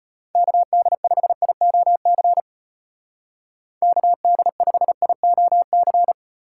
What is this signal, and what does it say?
Morse code, heard as a keyed tone, about 30 wpm, 700 Hz. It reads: KD5IOC KD5IOC
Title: Speed Racing - USA 2x3 Call Signs 25wpm